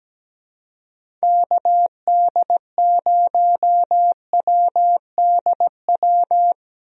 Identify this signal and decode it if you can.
Morse code, heard as a keyed tone, about 17 wpm, 700 Hz. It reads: KD0WDW
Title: USA Call Signs - 6 Characters - Repeated 3x - 17wpm